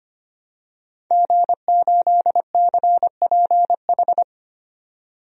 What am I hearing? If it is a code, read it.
Morse code, heard as a keyed tone, about 25 wpm, 690 Hz. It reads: G8CPH
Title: DXCC-100 Call Signs - 5 Characters - 25wpm